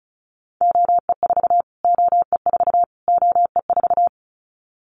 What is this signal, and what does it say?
Morse code, heard as a keyed tone, about 34 wpm, 700 Hz. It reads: OE4 OE4 OE4
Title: Callsign Prefixes for Top 25 DXCC - Repeated 3x - 35wpm